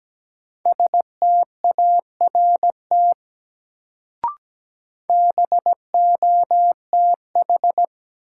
Morse code, heard as a keyed tone, about 17 wpm, 700 Hz. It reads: START BOTH
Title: Sets of 4 Words - New Word Drill - 17wpm